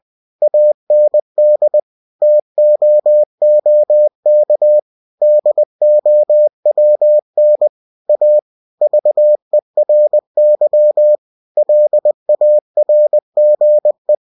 Morse code, heard as a keyed tone, about 20 wpm, 600 Hz. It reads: AND TOOK DOWN A VERY LARGE
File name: cw-books / winnie-the-pooh / winnie-20wpm-page-0037